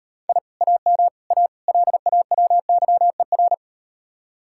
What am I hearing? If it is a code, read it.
Morse code, heard as a keyed tone, about 38 wpm, 700 Hz. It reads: I AM A LAWYER